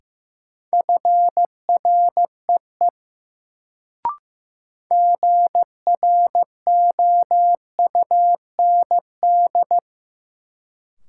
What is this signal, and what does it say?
Morse code, heard as a keyed tone, about 16 wpm, 700 Hz. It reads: FREE GROUND
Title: Sentences from Top 500 Words - New Word Drill - 15wpm